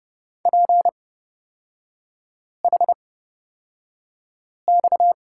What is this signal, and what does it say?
Morse code, heard as a keyed tone, about 30 wpm, spaced as for 5 wpm, 700 Hz. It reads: PHX
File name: Lesson 37e - Letter x - Three Characters 30wpm